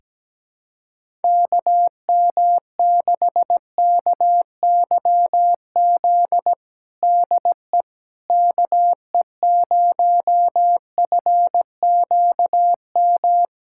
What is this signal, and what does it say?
Morse code, heard as a keyed tone, about 17 wpm, 700 Hz. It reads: KM6KYZ DE KE0FQM
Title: USA Callsign DE Callsign 17wpm